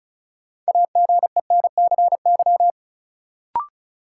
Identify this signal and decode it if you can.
Morse code, heard as a keyed tone, about 36 wpm, 700 Hz. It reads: AGENCY
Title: Sentences from Top 1300 Words - New Word Drill Alt - 35wpm